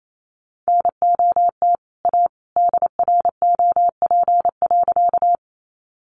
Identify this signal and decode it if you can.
Morse code, heard as a keyed tone, about 28 wpm, 700 Hz. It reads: NOT A DROP.
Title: Sentences from Top 1400 Words 28wpm